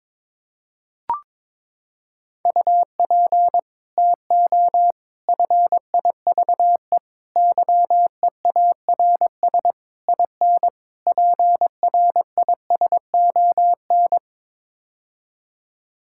Morse code, heard as a keyed tone, about 22 wpm, 700 Hz. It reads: UP TO FIVE YEARS IN PRISON
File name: practice / 22wpm / Sets of 6 Words - 22wpm